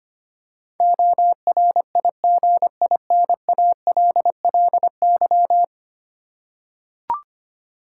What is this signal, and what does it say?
Morse code, heard as a keyed tone, about 25 wpm, 700 Hz. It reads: ORIGINALLY
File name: practice / 25wpm / Ten Letter Words - 25wpm